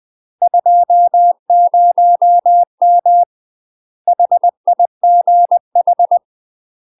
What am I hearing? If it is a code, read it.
Morse code, heard as a keyed tone, about 20 wpm, 700 Hz. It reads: 20M HIGH
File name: UK MASTER QSOs TRAINING / 20WPM / UK MASTER QSOs 0032